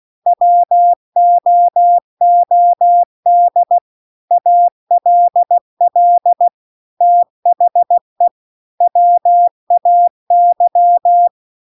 Morse code, heard as a keyed tone, about 16 wpm, 700 Hz. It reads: WOOD ALL THE WAY